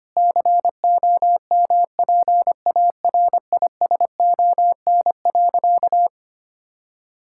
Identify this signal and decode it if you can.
Morse code, heard as a keyed tone, about 25 wpm, 700 Hz. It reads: COMPARISON.
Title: Sentences from Top 1600 Words 25wpm